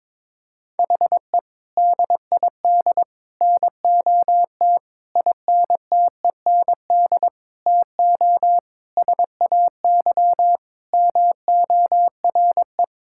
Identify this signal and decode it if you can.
Morse code, heard as a keyed tone, about 22 wpm, 700 Hz. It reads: HE DID NOT INTEND TO SAY MORE
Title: Sentences from Top 1600 Words 22wpm